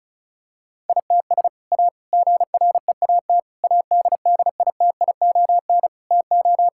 Morse code, heard as a keyed tone, about 34 wpm, 700 Hz. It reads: ITS A GREAT ADDITION TO
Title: Sentences from Top 1000 Words - Review Part 6 - 35wpm